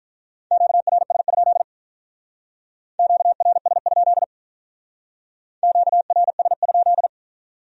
Morse code, heard as a keyed tone, about 45 wpm, 700 Hz. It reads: QRS? QRS? QRS?